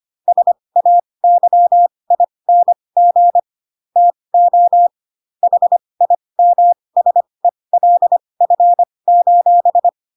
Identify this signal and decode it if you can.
Morse code, heard as a keyed tone, about 25 wpm, spaced as for 20 wpm, 700 Hz. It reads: SAYING TO HIMSELF:
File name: Pinocchio / Pinocchio ch6 15 25 ik1hge